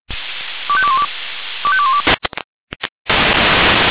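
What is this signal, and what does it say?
Five-tone selective call (ZVEI), heard as quick runs of five tones, tones of about 70 ms. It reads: 25212 25212
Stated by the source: ZVEI five-tone selective call received with an SDR.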